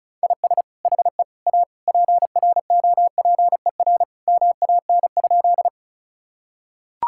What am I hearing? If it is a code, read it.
Morse code, heard as a keyed tone, about 36 wpm, 700 Hz. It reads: IS HE A PROPER MAN?
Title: Sentences from Top 1000 Words - Review Part 5 - 35wpm